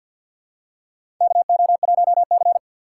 Morse code, heard as a keyed tone, about 53 wpm, 700 Hz. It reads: KO1C
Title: USA Call Signs - 4 Characters - 6x Character Spacing - 50wpm